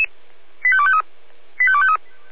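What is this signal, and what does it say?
Five-tone selective call (ZVEI), heard as quick runs of five tones, tones of about 70 ms. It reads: 85253 85253
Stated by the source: ZVEI five-tone selective call received on SDR.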